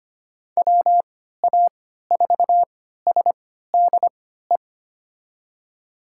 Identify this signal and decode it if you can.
Morse code, heard as a keyed tone, about 25 wpm, spaced as for 14 wpm, 700 Hz. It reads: WA4SDE
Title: USA Call Signs - 6 Characters - 3x Character Spacing - 25wpm